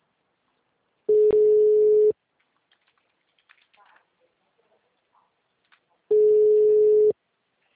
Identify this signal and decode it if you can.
Ringback tone (Europe, 425 Hz), with speech.